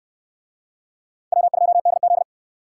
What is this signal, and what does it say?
Morse code, heard as a keyed tone, about 68 wpm, 700 Hz. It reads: W2DZ